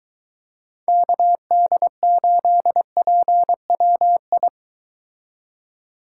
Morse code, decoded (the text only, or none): KD8PWI